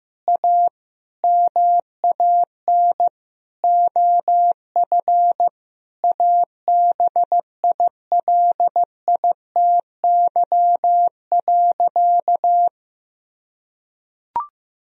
Morse code, heard as a keyed tone, about 15 wpm, 700 Hz. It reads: A MAN OF ABILITY.
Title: Sentences from Top 1000 Words - Review Part 3 - 15wpm